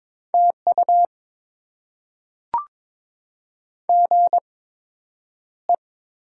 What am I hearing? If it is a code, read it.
Morse code, heard as a keyed tone, about 22 wpm, spaced as for 9 wpm, 700 Hz. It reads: TU GE
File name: Top 20 QSO Elements - 8x Character Spacing - 22wpm